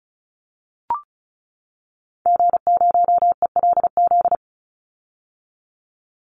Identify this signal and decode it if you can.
Morse code, heard as a keyed tone, about 34 wpm, 700 Hz. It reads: G0ELZ
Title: Call Signs - 35wpm